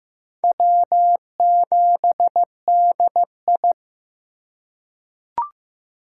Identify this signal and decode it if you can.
Morse code, heard as a keyed tone, about 15 wpm, 700 Hz. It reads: W7DI
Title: USA Call Signs - 4 Characters - 3x Character Spacing - 15wpm